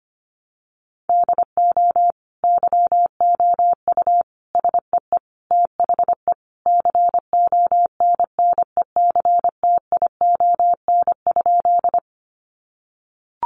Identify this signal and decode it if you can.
Morse code, heard as a keyed tone, about 25 wpm, 700 Hz. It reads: DO YOU SEE THE CONNECTION?
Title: Sentences from Top 1300 Words 25wpm